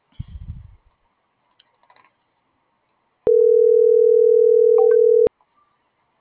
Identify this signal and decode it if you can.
Ringback tone (North America), 440 + 480 Hz.